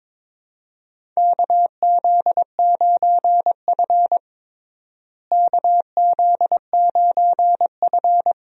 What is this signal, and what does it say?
Morse code, heard as a keyed tone, about 22 wpm, 700 Hz. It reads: KZ9F KZ9F
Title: USA Call Signs - 4 Characters - Repeated 2x - 22wpm